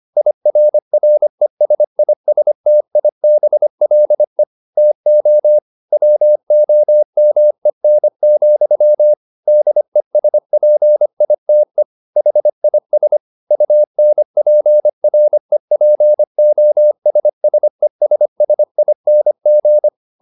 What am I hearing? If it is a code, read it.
Morse code, heard as a keyed tone, about 25 wpm, 600 Hz. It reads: IRRESISTIBLE TO WOMEN, DESPITE HIS UNPREPOSSESSING